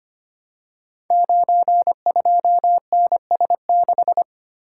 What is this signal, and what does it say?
Morse code, heard as a keyed tone, about 25 wpm, 700 Hz. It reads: 92NS6
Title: Code Groups - 5 Letter:Numbers - 25wpm